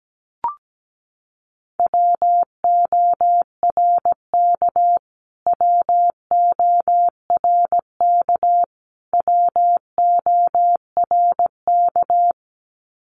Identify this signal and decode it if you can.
Morse code, heard as a keyed tone, about 17 wpm, 700 Hz. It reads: WORK WORK WORK
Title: Top 200 Words Repeated 17wpm